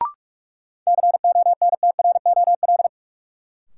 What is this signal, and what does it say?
Morse code, heard as a keyed tone, about 48 wpm, 700 Hz. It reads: CONTROL
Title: Sentences from Top 1200 Words - New Word Drill Alt - 45wpm